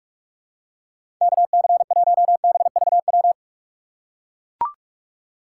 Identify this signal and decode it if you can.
Morse code, heard as a keyed tone, about 43 wpm, 700 Hz. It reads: KC1BVW
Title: USA Call Signs - 6 Characters - 4x Character Spacing - 45wpm